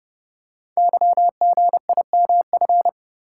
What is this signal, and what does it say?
Morse code, heard as a keyed tone, about 30 wpm, 700 Hz. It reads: YGIMF
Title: Code Groups - 5 Letter:Numbers - 30wpm